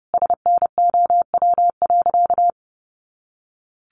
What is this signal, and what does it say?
Morse code, heard as a keyed tone, about 30 wpm, 700 Hz. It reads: SNOW.
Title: Easy Phrases 30wpm